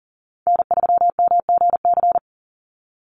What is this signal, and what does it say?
Morse code, heard as a keyed tone, about 40 wpm, 700 Hz. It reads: N3MGC
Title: USA Call Signs 40wpm